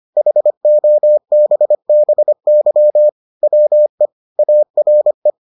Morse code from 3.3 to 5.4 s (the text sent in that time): WE ARE